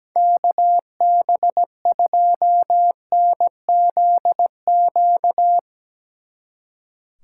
Morse code, heard as a keyed tone, about 18 wpm, 700 Hz. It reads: KB2NZQ
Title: Speed Racing - USA 2x3 Call Signs 17wpm